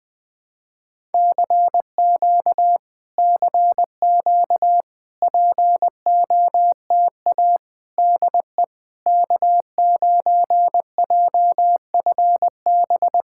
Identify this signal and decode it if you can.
Morse code, heard as a keyed tone, about 20 wpm, 690 Hz. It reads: CQ CQ POTA DE K9JFB
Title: POTA QSO Practice - 20wpm